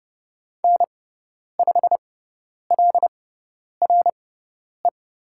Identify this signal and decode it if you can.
Morse code, heard as a keyed tone, about 30 wpm, spaced as for 10 wpm, 700 Hz. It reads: N5LRE
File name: Lesson 18g - Letter m - Call Signs 30wpm